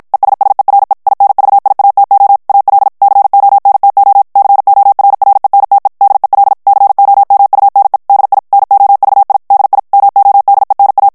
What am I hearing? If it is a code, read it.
Morse code, heard as a keyed tone, about 54 wpm, 800 Hz. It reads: ESSERE INVIATO AL CONTO CORRENTE DEL COMUNE DI NOVI DI MODENA
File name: morse mp3 / ITALIEN texte clair 1 vitesse 50 mots minute